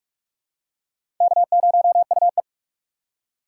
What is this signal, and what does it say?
Morse code, heard as a keyed tone, about 44 wpm, 700 Hz. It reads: K0UE
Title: USA Call Signs - 4 Characters - 4x Character Spacing - 45wpm